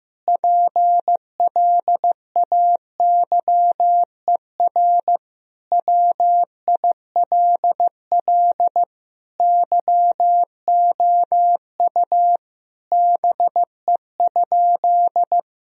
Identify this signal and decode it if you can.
Morse code, heard as a keyed tone, about 16 wpm, 700 Hz. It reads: PLAYER WILL YOU BE?
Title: Sentences from Top 1000 Words - Review Part 4 - 15wpm